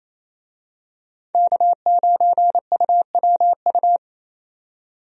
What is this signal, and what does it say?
Morse code, heard as a keyed tone, about 28 wpm, 700 Hz. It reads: K9UWU
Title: USA Call Signs - 5 Characters - 28wpm